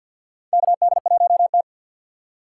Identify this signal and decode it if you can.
Morse code, heard as a keyed tone, about 49 wpm, 700 Hz. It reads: KD1T
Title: USA Call Signs - 4 Characters - 6x Character Spacing - 50wpm